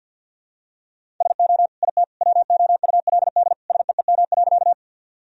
Morse code, heard as a keyed tone, about 46 wpm, 700 Hz. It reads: SO IT WOULD SEEM.
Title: Sentences from Top 1000 Words - Review Part 1 - 50wpm